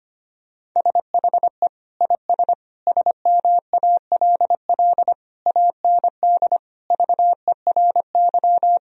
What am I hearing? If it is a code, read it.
Morse code, heard as a keyed tone, about 25 wpm, 700 Hz. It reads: SHE IS SMALL AND VERY
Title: Sentences from Top 1000 Words - Review Part 1 - 25wpm